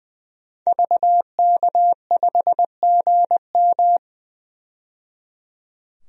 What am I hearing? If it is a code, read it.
Morse code, heard as a keyed tone, about 20 wpm, 700 Hz. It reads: VK5GM
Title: DXCC-100 Call Signs - 5 Characters - 20wpm